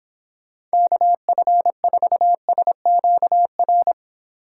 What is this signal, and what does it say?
Morse code, heard as a keyed tone, about 26 wpm, 700 Hz. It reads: KF4SQR